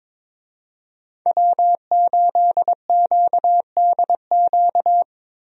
Morse code, heard as a keyed tone, about 22 wpm, 700 Hz. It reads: W8QDQ
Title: USA Call Signs - 5 Characters - Repeated 2x - 22wpm